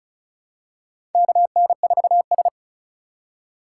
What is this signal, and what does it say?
Morse code, heard as a keyed tone, about 34 wpm, 700 Hz. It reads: KN4S
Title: USA Call Signs - 4 Characters - 4x Character Spacing - 35wpm